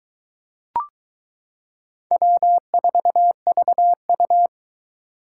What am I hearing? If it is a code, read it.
Morse code, heard as a keyed tone, about 22 wpm, 700 Hz. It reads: W4VU